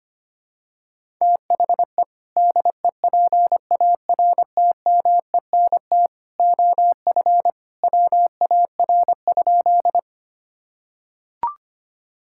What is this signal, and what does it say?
Morse code, heard as a keyed tone, about 25 wpm, 700 Hz. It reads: THE DEPARTMENT OF WAR?
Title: Sentences from Top 1000 Words - Review Part 5 - 25wpm